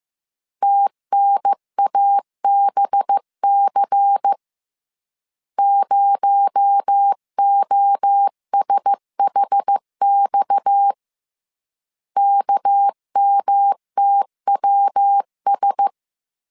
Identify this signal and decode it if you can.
Morse code, heard as a keyed tone, about 15 wpm, 800 Hz. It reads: TNABC 0OSHX KMTWS
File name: CW / DET / 1 6 1500C S C 14wpm 2951